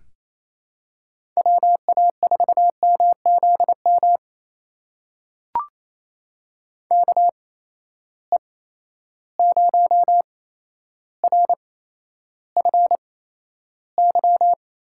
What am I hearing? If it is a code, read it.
Morse code, heard as a keyed tone, about 28 wpm, spaced as for 12 wpm, 700 Hz. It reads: WA4MZM KE0RFY
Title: USA Call Signs - 6 Characters - 8x Character Spacing - 28wpm